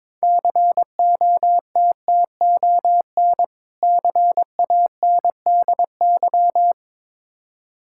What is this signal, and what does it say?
Morse code, heard as a keyed tone, about 22 wpm, 690 Hz. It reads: COTTON CANDY
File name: Difficult Words with Hints - 22wpm